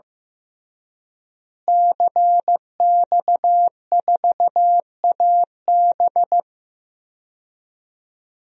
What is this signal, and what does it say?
Morse code, heard as a keyed tone, about 15 wpm, 700 Hz. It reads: CX4AB
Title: DXCC-100 Call Signs - 5 Characters - Repeated 3x - 15wpm